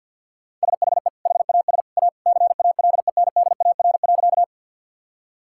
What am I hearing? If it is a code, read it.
Morse code, heard as a keyed tone, about 50 wpm, 700 Hz. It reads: SHE HAS A CALENDAR.
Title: Sentences from Top 1600 Words 50wpm